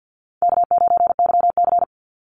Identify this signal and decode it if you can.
Morse code, heard as a keyed tone, about 50 wpm, 700 Hz. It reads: K9YC
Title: Call Signs Repeated 3x - 50wpm